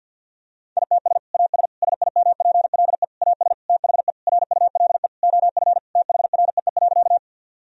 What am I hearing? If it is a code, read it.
Morse code, heard as a keyed tone, about 50 wpm, 700 Hz. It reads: ITS AS SIMPLE AS THE RULE OF THREE.
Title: Sentences from Top 1000 Words - Review Part 2 - 50wpm